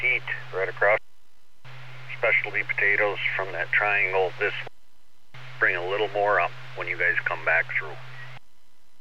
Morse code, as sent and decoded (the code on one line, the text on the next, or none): none